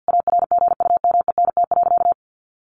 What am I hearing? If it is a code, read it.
Morse code, heard as a keyed tone, about 50 wpm, 700 Hz. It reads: ARGUMENT.
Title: Sentences from Top 1400 Words 50wpm